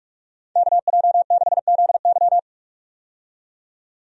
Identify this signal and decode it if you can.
Morse code, heard as a keyed tone, about 41 wpm, 700 Hz. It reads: KJ6ZY
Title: USA Call Signs - 5 Characters - 45wpm